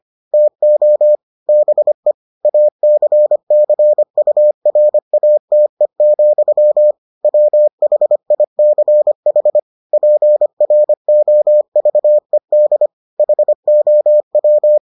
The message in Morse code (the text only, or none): TO BE ACCURATE, WHICH PROVED HOW